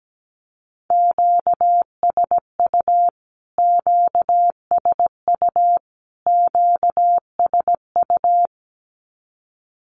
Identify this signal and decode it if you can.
Morse code, heard as a keyed tone, about 17 wpm, 700 Hz. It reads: QSU QSU QSU
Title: Q-Codes Repeated 3x 17wpm